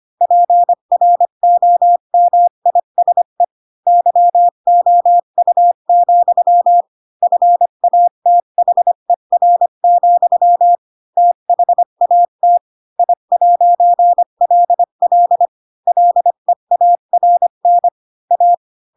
Morse code, heard as a keyed tone, about 25 wpm, 700 Hz. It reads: PROMISE YOU, FATHER, THAT I'LL LEARN A
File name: Pinocchio / Pinocchio ch8 20 25 ik1hge